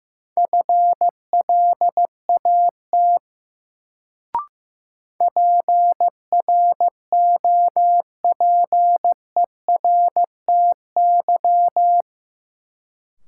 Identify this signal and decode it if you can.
Morse code, heard as a keyed tone, about 15 wpm, 700 Hz. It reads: FLAT PROPERTY
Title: Sentences from Top 1000 Words - New Word Drill - 15wpm